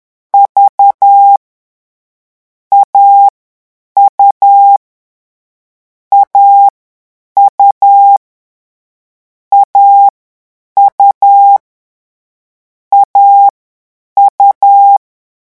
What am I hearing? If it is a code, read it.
Morse code, heard as a keyed tone, about 10 wpm, 800 Hz. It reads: V A U A U A U A U
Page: Leçon 3 - vitesse 10 mots minute